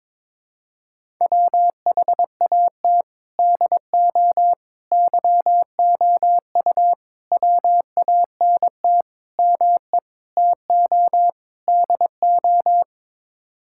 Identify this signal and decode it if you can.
Morse code, heard as a keyed tone, about 22 wpm, 700 Hz. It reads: WHAT DO YOU WANT ME TO DO
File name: practice / 22wpm / Sets of 7 Words - 22wpm